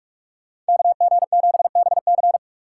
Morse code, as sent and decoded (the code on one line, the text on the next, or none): -.- --. --... -... -.-.
KG7BC